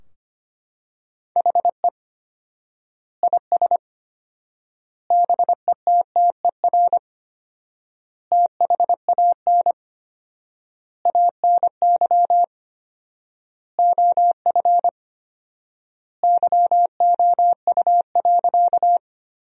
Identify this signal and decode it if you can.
Morse code, heard as a keyed tone, about 25 wpm, 700 Hz. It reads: HE IS BETTER THAN ANY OF YOU.
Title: 4x Word-Spacing - Sentences from Top 400 Words - 25wpm